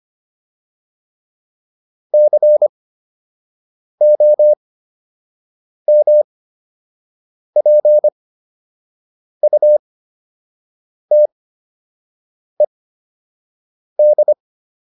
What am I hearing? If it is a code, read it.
Morse code, heard as a keyed tone, about 25 wpm, spaced as for 6 wpm, 600 Hz. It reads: COMPUTED